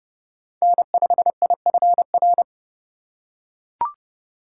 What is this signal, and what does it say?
Morse code, heard as a keyed tone, about 32 wpm, 690 Hz. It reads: N5IFR
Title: USA Call Signs - 5 Characters - 2x Character Spacing - 30wpm